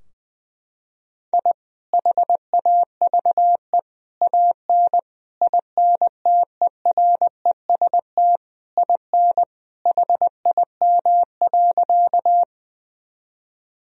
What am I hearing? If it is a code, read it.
Morse code, heard as a keyed tone, about 21 wpm, 700 Hz. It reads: I HAVE AN INTEREST IN HIM.
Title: Sentences from Top 1000 Words - Review Part 2 - 20wpm